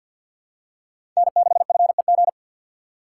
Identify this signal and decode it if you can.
Morse code, heard as a keyed tone, about 52 wpm, 700 Hz. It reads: N6FEG
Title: USA Call Signs - 5 Characters - 6x Character Spacing - 50wpm